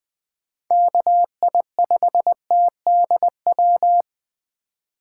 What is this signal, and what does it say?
Morse code, heard as a keyed tone, about 20 wpm, 700 Hz. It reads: KI5TDW
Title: 2023 POTA Activator Call Signs - 20wpm